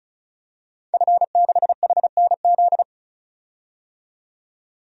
Morse code, heard as a keyed tone, about 36 wpm, 700 Hz. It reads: F6HNZ
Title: DXCC-100 Call Signs - 5 Characters - 35wpm